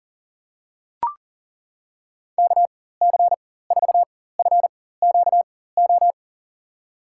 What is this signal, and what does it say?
Morse code, heard as a keyed tone, about 40 wpm, spaced as for 19 wpm, 700 Hz. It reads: KC4FQO
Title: USA Call Signs - 6 Characters - 4x Character Spacing - 40wpm